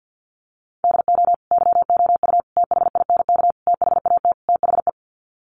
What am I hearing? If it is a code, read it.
Morse code, heard as a keyed tone, about 50 wpm, 700 Hz. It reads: DO YOU THINK THAT THE
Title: Sets of 5 Words Encore 50wpm